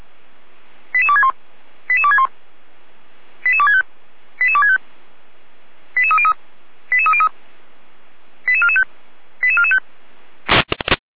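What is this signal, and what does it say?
Five-tone selective call (ZVEI), heard as quick runs of five tones, tones of about 70 ms. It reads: 88261 88261 88265 88265 88303 88303 88445 88445
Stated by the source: ZVEI five-tone call received on SDR.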